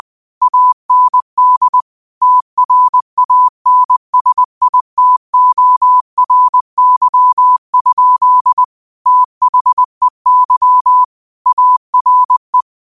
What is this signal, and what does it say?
Morse code, heard as a keyed tone, about 20 wpm, 1000 Hz. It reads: AND TRANSITORY? THEY ARE